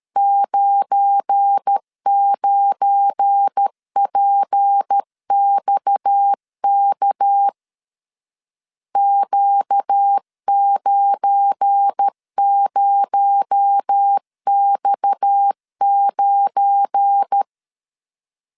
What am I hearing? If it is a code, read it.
Morse code, heard as a keyed tone, about 13 wpm, 800 Hz. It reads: 99PXK Q90X9
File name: LEZIONI / 7 GF9Z / 7 GF9 12wpm 2319